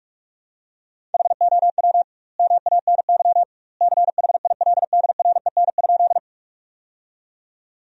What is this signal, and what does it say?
Morse code, heard as a keyed tone, about 44 wpm, 700 Hz. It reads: HOW MANY CHILDREN?
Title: 2x Word-Spacing - Sentences from Top 300 Words - 45wpm